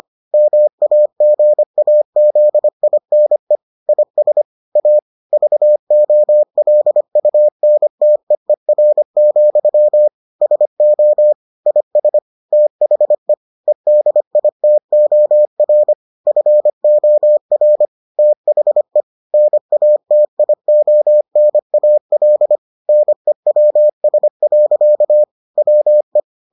Morse code, heard as a keyed tone, about 25 wpm, 600 Hz. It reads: MAGAZINE IS A VOLUNTEER, SO IS THE EDITOR FOR THE NATIONAL NEWS. WE